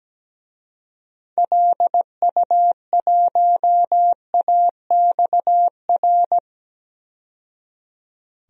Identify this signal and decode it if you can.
Morse code, heard as a keyed tone, about 17 wpm, 700 Hz. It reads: LU1AXR
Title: DXCC-100 Call Signs - 6 Characters - 17wpm